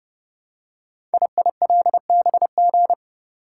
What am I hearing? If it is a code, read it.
Morse code, heard as a keyed tone, about 28 wpm, 690 Hz. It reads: IILBG
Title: Code Groups - 5 Letters - 30wpm